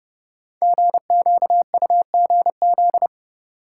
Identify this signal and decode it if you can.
Morse code, heard as a keyed tone, about 30 wpm, 700 Hz. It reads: GQUGZ
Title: Code Groups - 5 Letters - 30wpm